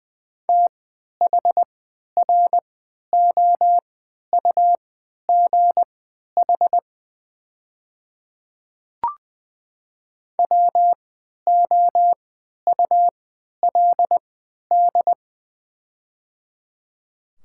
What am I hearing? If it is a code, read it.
Morse code, heard as a keyed tone, about 21 wpm, spaced as for 12 wpm, 700 Hz. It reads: THROUGH WOULD
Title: 3x Character Spacing - Top 200 Words - 20wpm